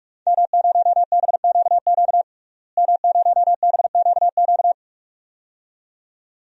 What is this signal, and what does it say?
Morse code, heard as a keyed tone, about 45 wpm, 700 Hz. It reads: M0BQQ M0BQQ
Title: DXCC-100 Call Signs - 5 Characters - Repeated 2x - 45wpm